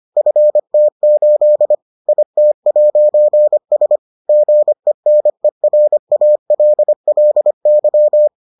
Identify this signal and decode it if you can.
Morse code, heard as a keyed tone, about 25 wpm, 600 Hz. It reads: FT8 IT'S GENERALLY